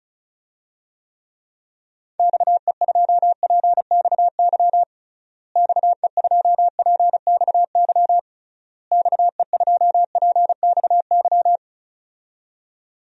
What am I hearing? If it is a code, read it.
Morse code, heard as a keyed tone, about 35 wpm, 700 Hz. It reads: XE2PXY XE2PXY XE2PXY
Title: DXCC-100 Call Signs - 6 Characters - Repeated 3x - 35wpm